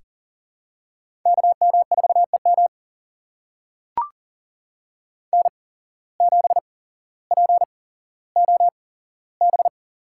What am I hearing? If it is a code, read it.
Morse code, heard as a keyed tone, about 40 wpm, spaced as for 17 wpm, 700 Hz. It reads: KM4EM N7POB
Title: USA Call Signs - 5 Characters - 8x Character Spacing - 40wpm